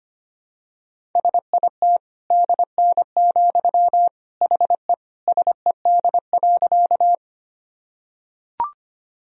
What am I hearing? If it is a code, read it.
Morse code, heard as a keyed tone, about 25 wpm, 700 Hz. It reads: SIT DN, HE SED.
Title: Sentences from Top 500 Words and Abbreviations 25wpm